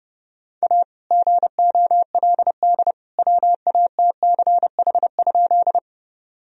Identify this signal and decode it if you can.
Morse code, heard as a keyed tone, about 32 wpm, 700 Hz. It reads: A GOLD WATCH?
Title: Sentences from Top 1000 Words - Review Part 3 - 30wpm